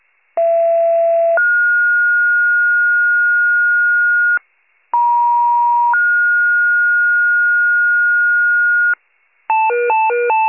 Two-tone page, heard as a one-second tone, then a three-second tone, about 670 then 1430 Hz, about 950 then 1430 Hz. It is heard from 0.4 to 8.9 s.